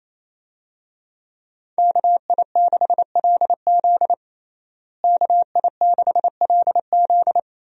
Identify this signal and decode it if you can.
Morse code, heard as a keyed tone, about 28 wpm, 700 Hz. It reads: KI6LZ KI6LZ
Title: USA Call Signs - 5 Characters - Repeated 3x - 28wpm